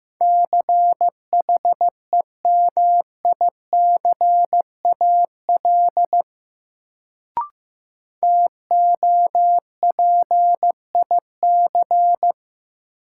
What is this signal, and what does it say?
Morse code, heard as a keyed tone, about 15 wpm, 700 Hz. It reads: CHEMICAL TOPIC
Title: Sentences from Top 1200 Words - New Word Drill - 15wpm